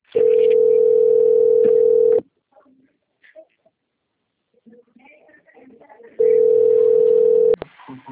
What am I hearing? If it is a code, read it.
Ringback tone (North America), 440 + 480 Hz.
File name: out- - - - - .10278